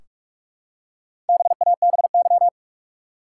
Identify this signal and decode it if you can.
Morse code, heard as a keyed tone, about 47 wpm, 700 Hz. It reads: BABY
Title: Four Letter Words - 45wpm